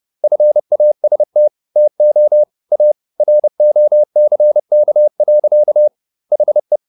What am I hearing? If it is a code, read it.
Morse code, heard as a keyed tone, about 30 wpm, 600 Hz. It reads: FAST TO A ROCK. HE